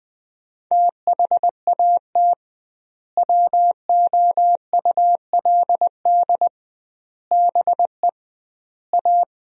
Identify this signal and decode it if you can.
Morse code, heard as a keyed tone, about 20 wpm, 700 Hz. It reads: THAT WOULD BE A
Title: 2x Word-Spacing - Sentences from Top 500 Words - 20wpm